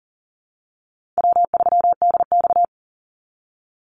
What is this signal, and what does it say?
Morse code, heard as a keyed tone, about 38 wpm, 700 Hz. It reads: W3DX
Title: NA SOTA Activator Callsigns 40wpm